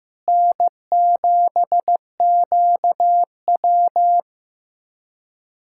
Morse code, heard as a keyed tone, about 15 wpm, 700 Hz. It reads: N7QW
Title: USA Callsign DE Callsign - 4 Characters - 15wpm